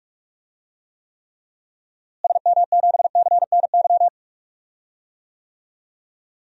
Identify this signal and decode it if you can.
Morse code, heard as a keyed tone, about 44 wpm, 700 Hz. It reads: SM7CNY
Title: DXCC-100 Call Signs - 6 Characters - Repeated 3x - 45wpm